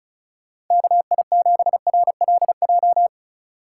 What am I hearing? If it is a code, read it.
Morse code, heard as a keyed tone, about 44 wpm, 700 Hz. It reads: KI7RLJ
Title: Speed Racing - USA 2x3 Call Signs 35wpm